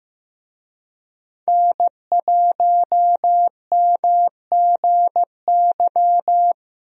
Morse code, heard as a keyed tone, about 15 wpm, 700 Hz. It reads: N1MGY
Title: USA Call Signs - 5 Characters - 15wpm